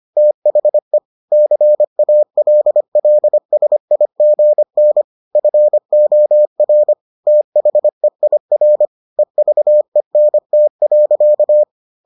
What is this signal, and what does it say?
Morse code, heard as a keyed tone, about 25 wpm, 600 Hz. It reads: THE CALLSIGN FOR THEIR EVENT.